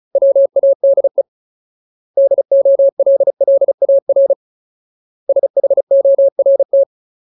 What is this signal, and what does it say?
Morse code, heard as a keyed tone, about 34 wpm, 550 Hz. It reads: WADE DOLLAR SHORT
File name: Ultimate Word List / 35WPM Ultimate